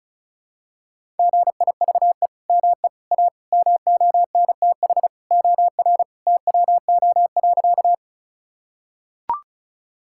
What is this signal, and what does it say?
Morse code, heard as a keyed tone, about 34 wpm, 700 Hz. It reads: GIVE ME A MONTH OR TWO.
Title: Sentences from Top 1000 Words - Review Part 5 - 35wpm